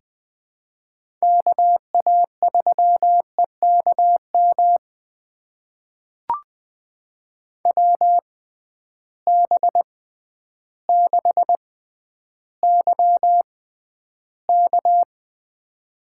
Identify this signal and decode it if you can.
Morse code, heard as a keyed tone, about 19 wpm, spaced as for 11 wpm, 700 Hz. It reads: KA3EKM WB6YK
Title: USA Call Signs - 6 Characters - 6x Character Spacing - 20wpm